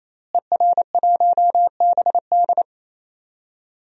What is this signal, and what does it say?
Morse code, heard as a keyed tone, about 28 wpm, 700 Hz. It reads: ER1BD
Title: DXCC-100 Call Signs - 5 Characters - Repeated 3x - 28wpm